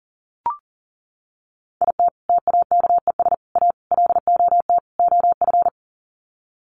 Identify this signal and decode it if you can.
Morse code, heard as a keyed tone, about 40 wpm, 700 Hz. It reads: IT TAKES A LOT OF
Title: Sets of 5 Words Encore 40wpm